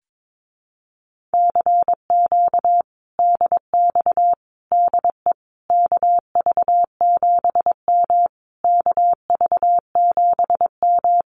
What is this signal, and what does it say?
Morse code, heard as a keyed tone, about 22 wpm, 700 Hz. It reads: CQ DX DE KV7M KV7M
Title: Common QSO Phrases Encore 22wpm